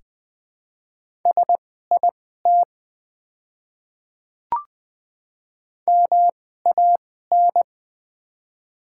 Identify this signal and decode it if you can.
Morse code, heard as a keyed tone, about 20 wpm, spaced as for 14 wpm, 700 Hz. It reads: SIT MAN
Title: 2x Character Spacing - Top 500 Words - 20wpm